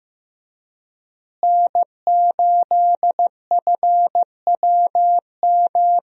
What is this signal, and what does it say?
Morse code, heard as a keyed tone, about 15 wpm, 700 Hz. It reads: N8FWM
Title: USA Call Signs - 5 Characters - Repeated 2x - 15wpm